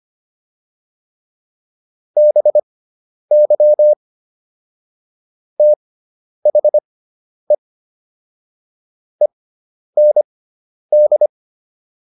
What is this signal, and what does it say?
Morse code, heard as a keyed tone, about 25 wpm, spaced as for 10 wpm, 600 Hz. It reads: BY THE END